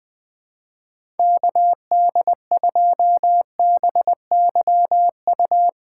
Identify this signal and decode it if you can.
Morse code, heard as a keyed tone, about 20 wpm, 700 Hz. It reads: KD2BYU
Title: USA Call Signs - 6 Characters - 2x Character Spacing - 20wpm